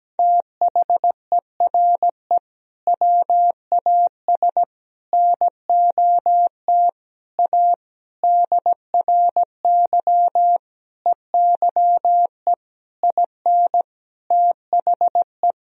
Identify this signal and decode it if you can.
Morse code, heard as a keyed tone, about 17 wpm, 700 Hz. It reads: THERE WAS NOT A DRY EYE IN THE
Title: Sentences from Top 1000 Words - Review Part 3 - 17wpm